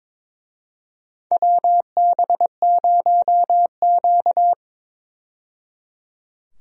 Morse code, heard as a keyed tone, about 22 wpm, 700 Hz. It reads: WB0Q
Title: USA Call Signs - 4 Characters - 22wpm